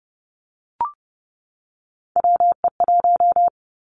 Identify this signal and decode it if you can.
Morse code, heard as a keyed tone, about 30 wpm, 700 Hz. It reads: WE1
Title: Three Challenging Letter-Number Combinations 30wpm